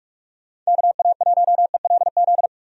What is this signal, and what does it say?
Morse code, heard as a keyed tone, about 41 wpm, 700 Hz. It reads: KA1ELZ
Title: USA Call Signs - 6 Characters - 45wpm